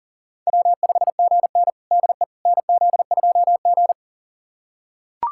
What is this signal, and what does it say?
Morse code, heard as a keyed tone, about 40 wpm, 700 Hz. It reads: W5GN DE NZ2G
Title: USA Callsign DE Callsign - 4 Characters - 40wpm